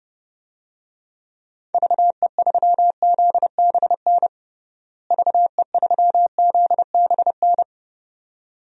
Morse code, heard as a keyed tone, about 30 wpm, 700 Hz. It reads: VE3ZBN VE3ZBN
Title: DXCC-100 Call Signs - 6 Characters - Repeated 2x - 30wpm